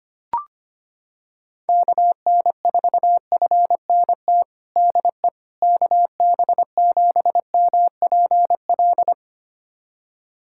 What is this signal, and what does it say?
Morse code, heard as a keyed tone, about 25 wpm, 700 Hz. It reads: KN4FNT DE KB7MPL
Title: USA Callsign DE Callsign - 6 Characters - 25wpm